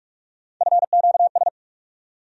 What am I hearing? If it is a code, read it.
Morse code, heard as a keyed tone, about 42 wpm, 700 Hz. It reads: FQS
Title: USA Call Sign Postfixes - 3 Letters 45wpm